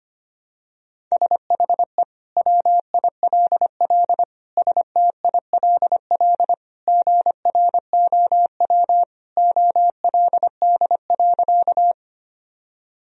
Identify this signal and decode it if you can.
Morse code, heard as a keyed tone, about 25 wpm, 700 Hz. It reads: SHE WILL STILL GROW OLD.
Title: Sentences from Top 1000 Words - Review Part 1 - 25wpm